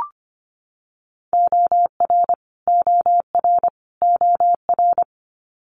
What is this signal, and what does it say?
Morse code, heard as a keyed tone, about 25 wpm, 700 Hz. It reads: OR OR OR
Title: Multi-letter Phonemes Repeated 3x 25wpm